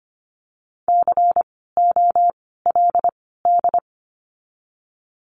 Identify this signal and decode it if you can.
Morse code, heard as a keyed tone, about 25 wpm, 700 Hz. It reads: C O L D
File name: course / 25:10wpm / 094 Lesson 16f - Letter c - Words 25wpm